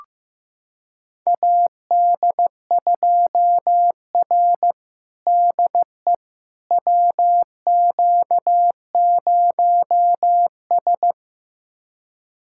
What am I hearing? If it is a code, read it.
Morse code, heard as a keyed tone, about 16 wpm, 700 Hz. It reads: AD2R DE WQ0S
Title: USA Callsign DE Callsign - 4 Characters - 15wpm